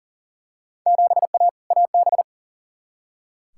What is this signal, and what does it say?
Morse code, heard as a keyed tone, about 40 wpm, 700 Hz. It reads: 7A AB
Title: Field Day Exchanges Encore - 40wpm